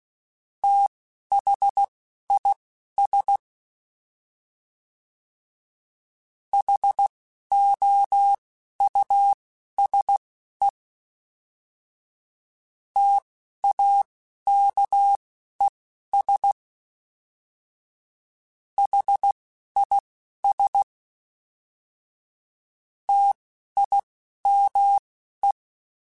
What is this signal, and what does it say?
Morse code, heard as a keyed tone, about 16 wpm, spaced as for 12 wpm, 800 Hz. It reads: THIS HOUSE TAKES HIS TIME
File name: T then H Runs / T then H Runs.mp3s / T then H Runs @10